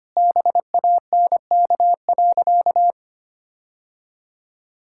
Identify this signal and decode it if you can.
Morse code, heard as a keyed tone, about 25 wpm, 700 Hz. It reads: BANK.
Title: Sentences from Top 1000 Words - Review Part 6 - 25wpm